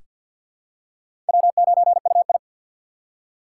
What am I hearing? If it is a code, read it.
Morse code, heard as a keyed tone, about 47 wpm, 700 Hz. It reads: W9UI
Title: USA Call Signs - 3x Character Spacing - 50wpm